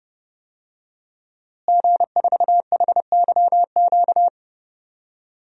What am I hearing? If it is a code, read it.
Morse code, heard as a keyed tone, about 30 wpm, 700 Hz. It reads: G4HYQ
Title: DXCC-100 Call Signs - 5 Characters - Repeated 2x - 30wpm